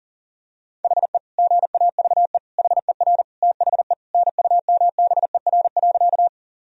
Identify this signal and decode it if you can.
Morse code, heard as a keyed tone, about 37 wpm, 700 Hz. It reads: HE GAVE HER THE NUMBER.
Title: Sentences from Top 1000 Words - Review Part 1 - 40wpm